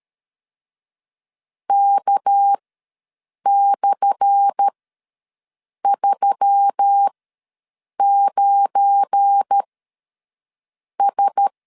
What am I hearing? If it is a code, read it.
Morse code, heard as a keyed tone, about 13 wpm, spaced as for 6 wpm, 800 Hz. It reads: K/39S
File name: LEZIONI / 10 SIGNS / 10A SLASH ws 12wpm 3443